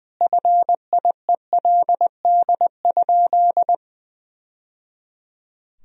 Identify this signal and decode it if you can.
Morse code, heard as a keyed tone, about 20 wpm, 700 Hz. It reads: FIELD?
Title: Sentences from Top 1000 Words - Review Part 2 - 20wpm